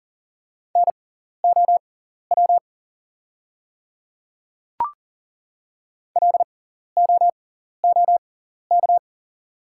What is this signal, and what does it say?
Morse code, heard as a keyed tone, about 40 wpm, spaced as for 14 wpm, 690 Hz. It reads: NOW LOOK
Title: 6x Character Spacing - Top 200 Words - 40wpm